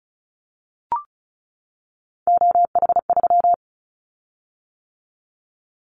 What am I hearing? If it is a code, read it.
Morse code, heard as a keyed tone, about 36 wpm, 700 Hz. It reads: OH3
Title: Callsign Prefixes for Top 25 DXCC - 35wpm